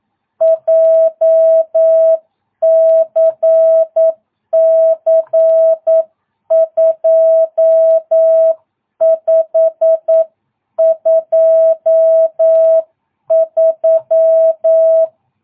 Morse code, heard as a keyed tone, about 9 wpm, 660 Hz. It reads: JCC 2523
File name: jh8cbh / 2022cw